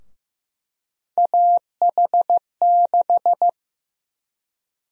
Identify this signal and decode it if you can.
Morse code, heard as a keyed tone, about 15 wpm, 700 Hz. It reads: AH6
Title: USA Call Sign Prefixes - Repeated 2x - 15wpm